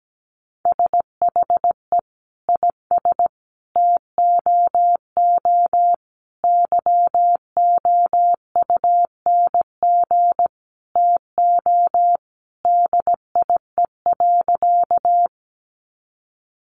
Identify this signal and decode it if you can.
Morse code, heard as a keyed tone, about 17 wpm, 700 Hz. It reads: SHE IS TOO YOUNG TO DIE.
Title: Sentences from Top 1300 Words 17wpm